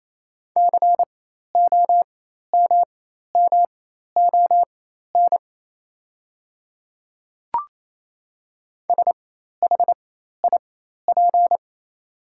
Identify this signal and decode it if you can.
Morse code, heard as a keyed tone, about 27 wpm, spaced as for 13 wpm, 690 Hz. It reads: COMMON SHIP
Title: 4x Character Spacing - Top 500 Words - 28wpm